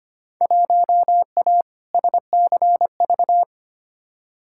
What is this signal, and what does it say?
Morse code, heard as a keyed tone, about 25 wpm, 700 Hz. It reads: 1A SCV
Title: Field Day Exchanges - 25wpm